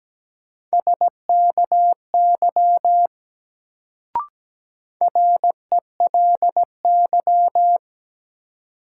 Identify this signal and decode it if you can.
Morse code, heard as a keyed tone, about 17 wpm, 700 Hz. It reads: SKY RELY
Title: Sentences from Top 1200 Words - New Word Drill - 17wpm